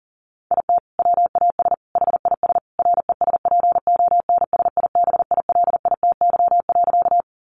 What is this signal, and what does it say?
Morse code, heard as a keyed tone, about 40 wpm, 700 Hz. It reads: IT WAS HIS RESPONSIBILITY.
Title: Sentences from Top 1200 Words 40wpm